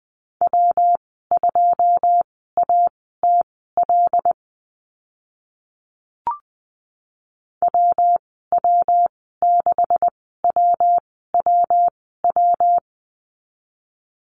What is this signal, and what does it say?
Morse code, heard as a keyed tone, about 20 wpm, spaced as for 14 wpm, 700 Hz. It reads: W2ATL WW6WWW
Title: 140 Lesson 23g - Letter f - Call Signs 20wpm